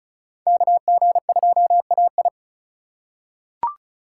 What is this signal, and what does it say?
Morse code, heard as a keyed tone, about 36 wpm, 700 Hz. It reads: KG2AI
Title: USA Call Signs - 5 Characters - 2x Character Spacing - 35wpm